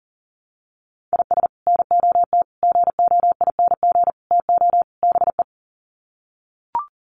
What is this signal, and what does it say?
Morse code, heard as a keyed tone, about 40 wpm, 700 Hz. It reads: IS NOT GOING TO BE
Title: Sets of 5 Words 40wpm